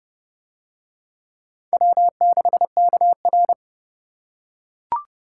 Morse code, heard as a keyed tone, about 29 wpm, 700 Hz. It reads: W6KR